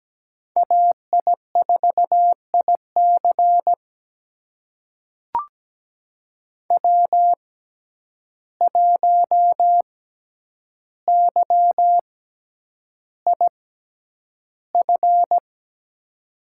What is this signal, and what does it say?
Morse code, heard as a keyed tone, about 17 wpm, spaced as for 9 wpm, 700 Hz. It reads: AI4IC W1YIF
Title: USA Call Signs - 5 Characters - 6x Character Spacing - 17wpm